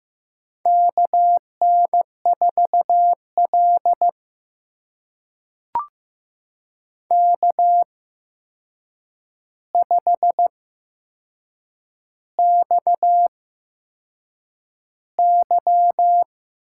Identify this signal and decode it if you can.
Morse code, heard as a keyed tone, about 15 wpm, spaced as for 6 wpm, 700 Hz. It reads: KN4L K5XY
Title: USA Call Signs - 4 Characters - 8x Character Spacing - 15wpm